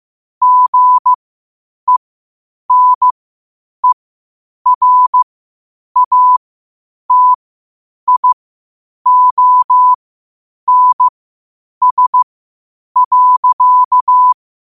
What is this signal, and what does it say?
Morse code, heard as a keyed tone, about 15 wpm, spaced as for 8 wpm, 1000 Hz. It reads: GENERATIONS.
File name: qotdfiles / 8wpm